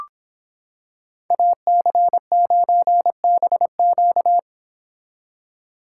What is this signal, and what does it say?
Morse code, heard as a keyed tone, about 26 wpm, 700 Hz. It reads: AC9BQ